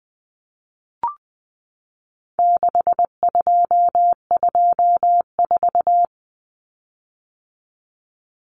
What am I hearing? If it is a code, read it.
Morse code, heard as a keyed tone, about 20 wpm, 700 Hz. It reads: 6224
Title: Common Cut Numbers - 4-Digit 20wpm